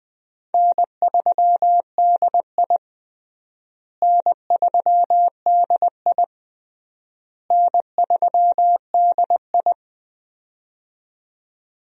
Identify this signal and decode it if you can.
Morse code, heard as a keyed tone, about 20 wpm, 700 Hz. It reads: N3DI N3DI N3DI
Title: USA Call Signs - 4 Characters - Repeated 3x - 20wpm